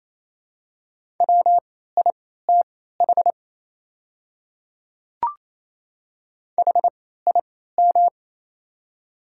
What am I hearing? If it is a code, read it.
Morse code, heard as a keyed tone, about 28 wpm, spaced as for 16 wpm, 700 Hz. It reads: WITH HIM
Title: 3x Character Spacing - Top 300 Words - 28wpm